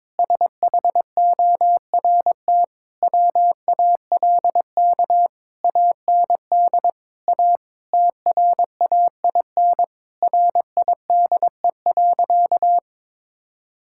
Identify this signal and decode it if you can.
Morse code, heard as a keyed tone, about 22 wpm, 700 Hz. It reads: SHORT WALK AND A TRAIN RIDE.
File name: Difficult Sentences with Hints - 22wpm